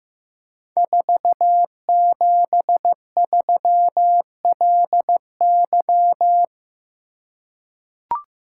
Morse code, heard as a keyed tone, about 15 wpm, 700 Hz. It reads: 473LY
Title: Code Groups - 5 Letter:Numbers - 15wpm